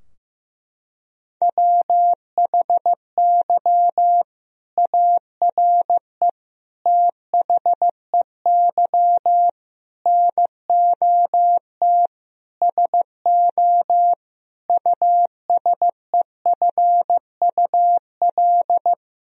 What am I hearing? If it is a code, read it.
Morse code, heard as a keyed tone, about 15 wpm, 700 Hz. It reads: WHY ARE THEY NOT SO USEFUL
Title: Sentences from Top 1000 Words - Review Part 4 - 15wpm